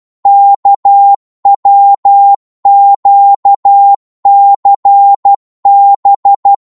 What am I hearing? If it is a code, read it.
Morse code, heard as a keyed tone, about 12 wpm, 800 Hz. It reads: KWQCB
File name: Morse / ltr-12wpm